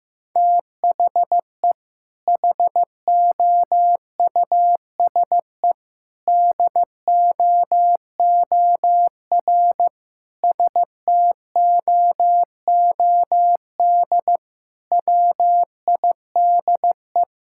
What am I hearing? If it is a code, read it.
Morse code, heard as a keyed tone, about 15 wpm, 700 Hz. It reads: THE HOUSE DOOR STOOD WIDE
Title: Sentences from Top 1000 Words - Review Part 6 - 15wpm